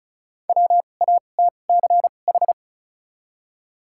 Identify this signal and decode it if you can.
Morse code, heard as a keyed tone, about 34 wpm, spaced as for 25 wpm, 700 Hz. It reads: WATCH
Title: 2x Character Spacing - Top 300 Words - 35wpm